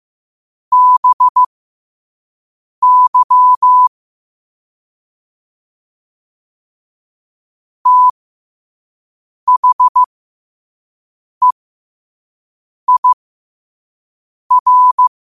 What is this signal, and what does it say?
Morse code, heard as a keyed tone, about 15 wpm, spaced as for 5 wpm, 1000 Hz. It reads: BY THEIR
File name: qotdfiles / 5wpm